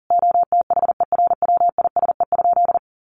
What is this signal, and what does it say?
Morse code, heard as a keyed tone, about 40 wpm, 700 Hz. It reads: OTHERWISE?
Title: Sentences from Top 1200 Words 40wpm